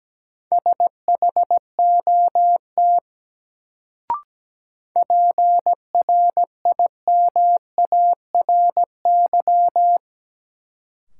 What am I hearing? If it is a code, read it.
Morse code, heard as a keyed tone, about 17 wpm, 700 Hz. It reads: SHOT PRIMARY
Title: Sentences from Top 1100 Words - New Word Drill Alt - 17wpm